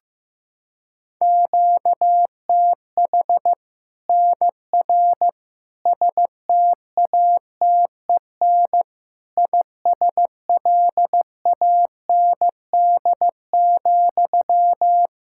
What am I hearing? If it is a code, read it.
Morse code, heard as a keyed tone, about 15 wpm, 700 Hz. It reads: QTH NR STATEN ISLAND,
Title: Practice QSO - Exchange 1 - Beginning - 15wpm